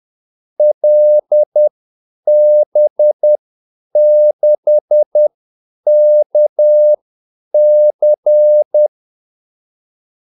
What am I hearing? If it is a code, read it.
Morse code, heard as a keyed tone, about 10 wpm, spaced as for 8 wpm, 600 Hz. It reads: LB6KC
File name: Callsigns-10-8wpm